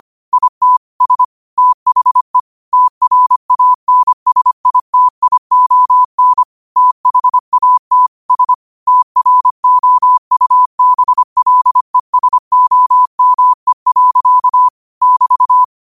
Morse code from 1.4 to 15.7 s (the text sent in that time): THE TRANSITION THAT S TROUBLESOME. <BT>